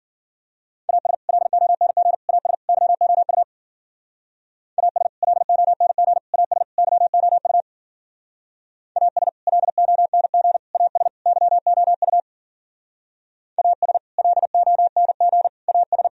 Morse code, heard as a keyed tone, about 52 wpm, 700 Hz. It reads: AS LONG AS YOU AS LONG AS YOU AS LONG AS YOU AS LONG AS